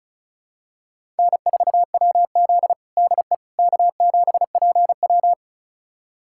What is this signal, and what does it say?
Morse code, heard as a keyed tone, about 36 wpm, 700 Hz. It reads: N4WZ DE K7PW
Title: USA Callsign DE Callsign - 4 Characters - 35wpm